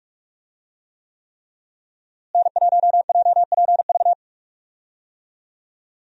Morse code, heard as a keyed tone, about 45 wpm, 700 Hz. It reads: N1JPV
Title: USA Call Signs - 5 Characters - Repeated 3x - 45wpm